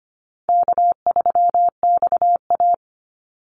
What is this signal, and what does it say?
Morse code, heard as a keyed tone, about 25 wpm, 700 Hz. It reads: K3XA
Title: Easy Call Signs - 25wpm